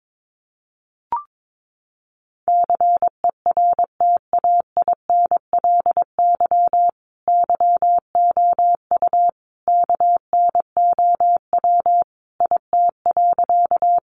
Morse code, heard as a keyed tone, about 22 wpm, 700 Hz. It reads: CERTAINLY YOU KNOW IT.
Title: Sentences from Top 900 Words 22wpm